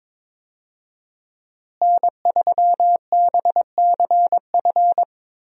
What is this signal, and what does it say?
Morse code, heard as a keyed tone, about 22 wpm, 700 Hz. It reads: N3BCF
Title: USA Call Signs - 5 Characters - Repeated 3x - 22wpm